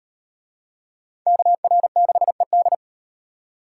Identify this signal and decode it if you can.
Morse code, heard as a keyed tone, about 37 wpm, 700 Hz. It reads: KR6ED